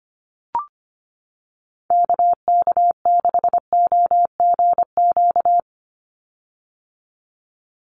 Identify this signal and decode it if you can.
Morse code, heard as a keyed tone, about 25 wpm, 700 Hz. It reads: KK6OGQ
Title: USA Call Signs 25wpm